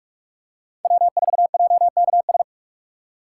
Morse code, heard as a keyed tone, about 44 wpm, 700 Hz. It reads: W 4 J K S
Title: USA Call Signs - 5 Characters - 6x Character Spacing - 45wpm